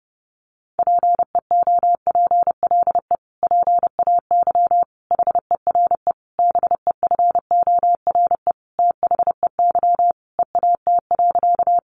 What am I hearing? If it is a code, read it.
Morse code, heard as a keyed tone, about 30 wpm, 700 Hz. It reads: PEOPLE PAY HERE BEFORE THEY EAT.
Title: Sentences from Top 700 Words 30wpm